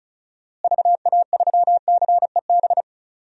Morse code, heard as a keyed tone, about 36 wpm, 700 Hz. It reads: VA3CEB
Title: DXCC-100 Call Signs - 6 Characters - Repeated 2x - 35wpm